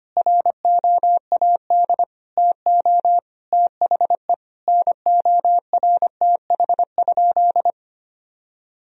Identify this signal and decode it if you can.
Morse code, heard as a keyed tone, about 25 wpm, 700 Hz. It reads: ROAD TO THE NORTH?
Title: Sentences from Top 1000 Words - Review Part 2 - 25wpm